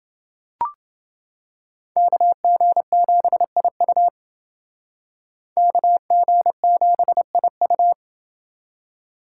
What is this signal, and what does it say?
Morse code, heard as a keyed tone, about 29 wpm, 700 Hz. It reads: KG7IU KG7IU